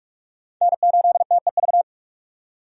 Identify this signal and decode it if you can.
Morse code, heard as a keyed tone, about 46 wpm, 700 Hz. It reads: N8TEV
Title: USA Call Signs - 5 Characters - 45wpm